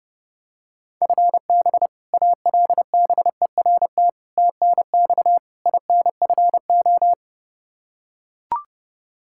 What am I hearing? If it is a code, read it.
Morse code, heard as a keyed tone, about 30 wpm, 700 Hz. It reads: FB ALBERT TNX INFO
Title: Practice QSO - Exchange 3 - More Info - 30wpm